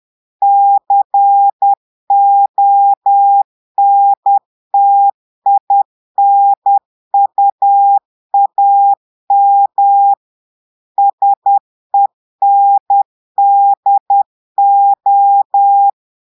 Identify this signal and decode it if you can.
Morse code, heard as a keyed tone, about 10 wpm, 800 Hz. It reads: CONTINUAM SENDO
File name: portugais texte en clair 2 vitesse 10 mots minutes